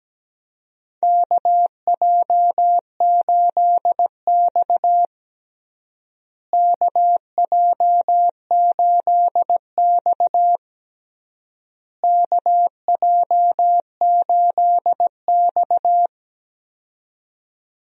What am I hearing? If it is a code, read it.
Morse code, heard as a keyed tone, about 17 wpm, 700 Hz. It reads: KJ8X KJ8X KJ8X
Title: USA Call Signs - 4 Characters - Repeated 3x - 17wpm